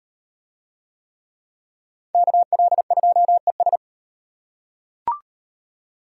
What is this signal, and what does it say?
Morse code, heard as a keyed tone, about 37 wpm, 700 Hz. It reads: KL2ES